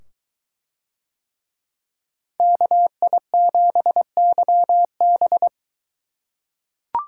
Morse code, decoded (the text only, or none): KI7YB